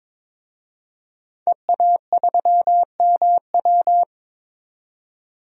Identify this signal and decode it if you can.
Morse code, heard as a keyed tone, about 22 wpm, 700 Hz. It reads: EA3MW
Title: DXCC-100 Call Signs - 5 Characters - 22wpm